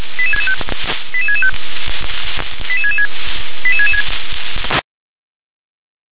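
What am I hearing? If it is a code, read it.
Five-tone selective call (ZVEI), heard as quick runs of five tones, tones of about 70 ms. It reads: 88665 88666 88666
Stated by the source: ZVEI five-tone selective call received by SDR.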